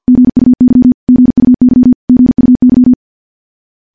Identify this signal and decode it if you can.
Morse code, heard as a keyed tone, about 51 wpm, 250 Hz. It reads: GUY GUY GUY
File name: morse / morse-training-50WPM